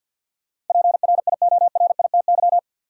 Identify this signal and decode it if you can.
Morse code, heard as a keyed tone, about 52 wpm, 700 Hz. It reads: PRIORITY
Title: Sentences from Top 1600 Words - New Word Drill Alt - 50wpm